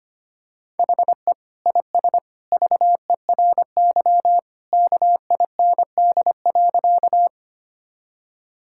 Morse code, heard as a keyed tone, about 25 wpm, 700 Hz. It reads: HE IS VERY KIND.
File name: practice / 25wpm / Sentences with 2-Word Phrases - 25wpm